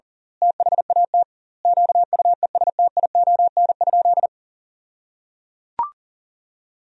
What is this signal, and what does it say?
Morse code, heard as a keyed tone, about 37 wpm, 700 Hz. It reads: THAT QUESTION?
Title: 2x Word-Spacing - Sentences from Top 400 Words - 40wpm